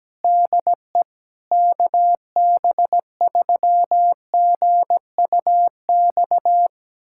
Morse code, heard as a keyed tone, about 17 wpm, 700 Hz. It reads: DE KB3GUX
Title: USA Callsign DE Callsign - 6 Characters - 17wpm